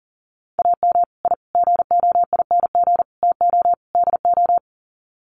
Morse code, heard as a keyed tone, about 40 wpm, 700 Hz. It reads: AM I GOING TO DO
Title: Sets of 5 Words Encore 40wpm